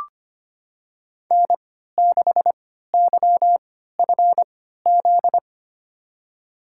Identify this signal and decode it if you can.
Morse code, heard as a keyed tone, about 25 wpm, spaced as for 14 wpm, 700 Hz. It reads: N6YFZ
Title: USA Call Signs - 5 Characters - 3x Character Spacing - 25wpm